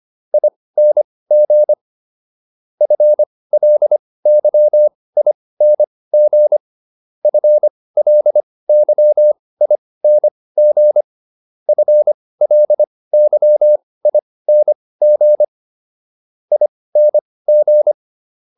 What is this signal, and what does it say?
Morse code, heard as a keyed tone, about 25 wpm, spaced as for 18 wpm, 600 Hz. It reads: ING FLYING FLYING FLYING ING